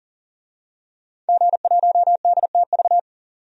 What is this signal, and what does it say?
Morse code, heard as a keyed tone, about 40 wpm, 700 Hz. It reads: G1DTV
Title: DXCC-100 Call Signs - 5 Characters - 40wpm